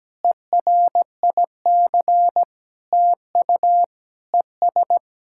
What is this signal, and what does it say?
Morse code, heard as a keyed tone, about 17 wpm, 700 Hz. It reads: ERIC TU ES
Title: SOTA QSO Practice 17wpm